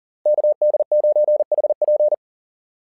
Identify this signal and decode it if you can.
Morse code, heard as a keyed tone, about 40 wpm, 600 Hz. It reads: KD9HP
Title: SOTA QSO Practice 40wpm